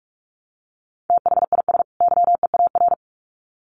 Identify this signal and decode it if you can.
Morse code, heard as a keyed tone, about 44 wpm, 700 Hz. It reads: THIS YEAR
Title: Sets of 2 Words 45wpm